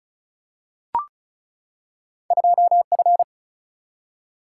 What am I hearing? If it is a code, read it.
Morse code, heard as a keyed tone, about 36 wpm, 700 Hz. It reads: 2 F
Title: Single-Letter Single-Number 35wpm